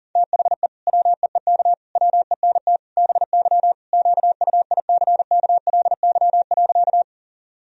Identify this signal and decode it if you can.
Morse code, heard as a keyed tone, about 40 wpm, 700 Hz. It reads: THE WEEK WENT BY QUICKLY.
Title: Sentences from Top 1000 Words - Review Part 2 - 40wpm